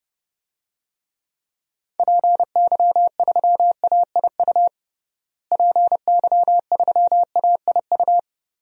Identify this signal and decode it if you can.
Morse code, heard as a keyed tone, about 32 wpm, 700 Hz. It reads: PY3AIU PY3AIU
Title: DXCC-100 Call Signs - 6 Characters - Repeated 3x - 30wpm